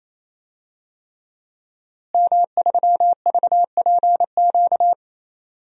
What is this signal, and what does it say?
Morse code, heard as a keyed tone, about 28 wpm, 700 Hz. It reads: M3VPQ
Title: DXCC-100 Call Signs - 5 Characters - Repeated 3x - 28wpm